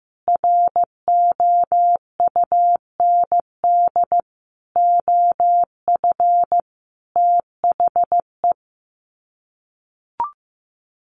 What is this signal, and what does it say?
Morse code, heard as a keyed tone, about 14 wpm, 700 Hz. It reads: ROUND OF THE
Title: Sets of 5 Words Encore 15wpm